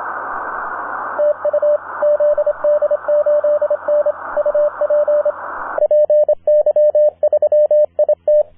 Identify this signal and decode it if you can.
Morse code, heard as a keyed tone, about 26 wpm, 600 Hz, with radio noise. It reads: TU ZD8N UP PY3IT
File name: zd8n-10m-band-cw